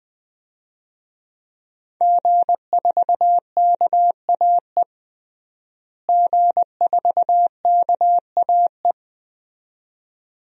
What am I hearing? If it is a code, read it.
Morse code, heard as a keyed tone, about 20 wpm, 700 Hz. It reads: G4KAE G4KAE
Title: DXCC-100 Call Signs - 5 Characters - Repeated 2x - 20wpm